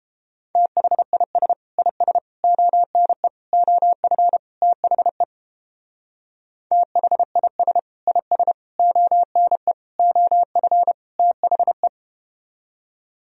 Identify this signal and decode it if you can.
Morse code, heard as a keyed tone, about 31 wpm, 700 Hz. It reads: THIS IS ONE OF THE THIS IS ONE OF THE